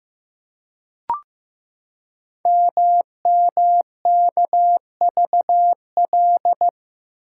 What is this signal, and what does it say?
Morse code, heard as a keyed tone, about 15 wpm, 700 Hz. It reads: MMKVL
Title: Code Groups - 5 Letters - 15wpm